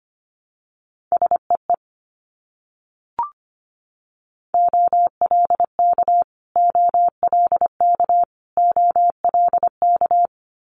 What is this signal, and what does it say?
Morse code, heard as a keyed tone, about 25 wpm, 700 Hz. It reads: SEE OLK OLK OLK
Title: Multi-letter Phonemes Repeated 3x 25wpm